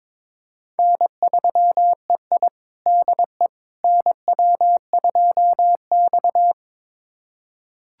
Morse code, heard as a keyed tone, about 22 wpm, 700 Hz. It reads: N3EI DE NW2X
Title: USA Callsign DE Callsign - 4 Characters - 22wpm